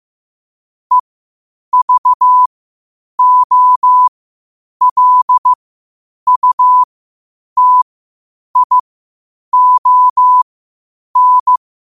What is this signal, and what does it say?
Morse code, heard as a keyed tone, about 15 wpm, spaced as for 8 wpm, 1000 Hz. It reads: EVOLUTION
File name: qotdfiles / 8wpm